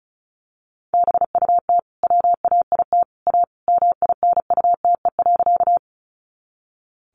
Morse code, heard as a keyed tone, about 34 wpm, 700 Hz. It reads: BUT WAIT A MINUTE.
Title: Sentences from Top 500 Words 35wpm